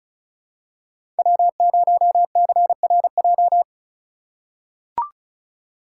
Morse code, heard as a keyed tone, about 36 wpm, 700 Hz. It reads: W0CRJ
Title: USA Call Signs - 5 Characters - 3x Character Spacing - 35wpm